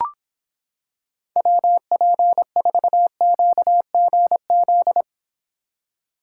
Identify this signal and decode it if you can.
Morse code, heard as a keyed tone, about 26 wpm, 700 Hz. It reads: WP4QGZ